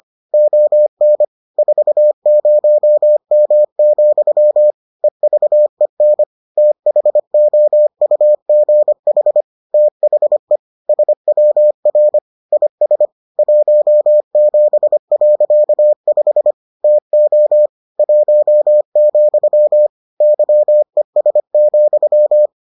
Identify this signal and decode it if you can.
Morse code, heard as a keyed tone, about 25 wpm, 600 Hz. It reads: ON 40M, EVEN THOUGH THE SWR IS 17.5 TO 1, YES,